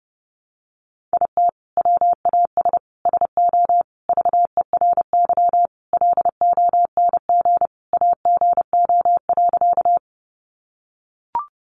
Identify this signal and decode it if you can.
Morse code, heard as a keyed tone, about 30 wpm, 700 Hz. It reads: IT WAS SO VERY LONG AGO.
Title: Sentences from Top 500 Words 30wpm